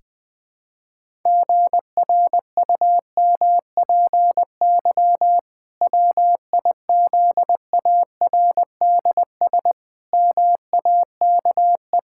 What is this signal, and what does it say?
Morse code, heard as a keyed tone, about 20 wpm, 700 Hz. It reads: GRUMPY WIZARDS MAKE
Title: Pangram Challenge v2 - 20wpm